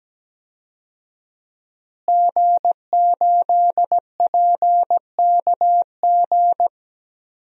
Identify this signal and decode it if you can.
Morse code, heard as a keyed tone, about 17 wpm, 700 Hz. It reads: G8PKG
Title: DXCC-100 Call Signs - 5 Characters - Repeated 3x - 17wpm